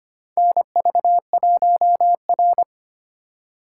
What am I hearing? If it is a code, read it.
Morse code, heard as a keyed tone, about 25 wpm, 690 Hz. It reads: NV1R
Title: USA Call Signs - 4x Character Spacing - 25wpm